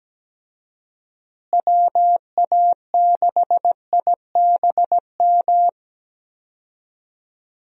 Morse code, heard as a keyed tone, about 17 wpm, 700 Hz. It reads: WA6IBM
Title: USA Call Signs - 6 Characters - Repeated 2x - 17wpm